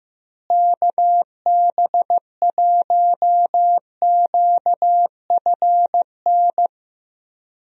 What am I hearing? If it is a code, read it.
Morse code, heard as a keyed tone, about 15 wpm, 700 Hz. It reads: KB1QFN
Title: USA Call Signs - 6 Characters - Repeated 2x - 15wpm